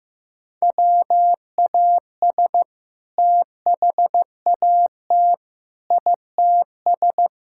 Morse code, heard as a keyed tone, about 15 wpm, 700 Hz. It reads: WAS THAT ITS
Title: Sentences from Top 1000 Words - Review Part 5 - 15wpm